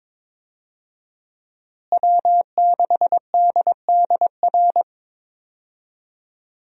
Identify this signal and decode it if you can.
Morse code, heard as a keyed tone, about 22 wpm, 700 Hz. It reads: W6DDR
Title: USA Call Signs - 5 Characters - Repeated 3x - 22wpm